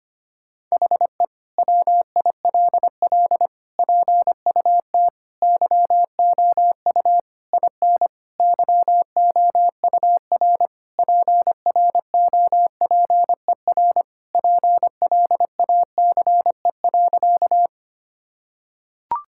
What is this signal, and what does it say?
Morse code, heard as a keyed tone, about 25 wpm, 700 Hz. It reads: HE WILL PUT YOU IN YOUR PROPER PLACE.
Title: Sentences from Top 1000 Words - Review Part 5 - 25wpm